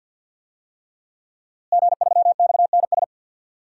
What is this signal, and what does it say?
Morse code, heard as a keyed tone, about 51 wpm, 700 Hz. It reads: G3XNS
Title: DXCC-100 Call Signs - 5 Characters - Repeated 3x - 50wpm